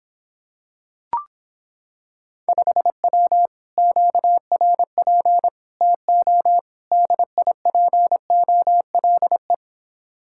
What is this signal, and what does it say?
Morse code, heard as a keyed tone, about 26 wpm, 700 Hz. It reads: 5W QRP TO DIPOLE